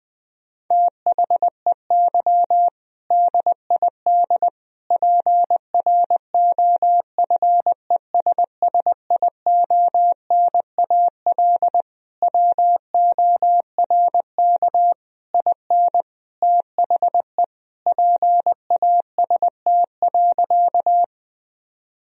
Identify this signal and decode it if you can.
Morse code, heard as a keyed tone, about 20 wpm, 700 Hz. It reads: THEY DID PROFESSIONAL WORK IN THE PAST.
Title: Sentences from Top 1000 Words - Review Part 4 - 20wpm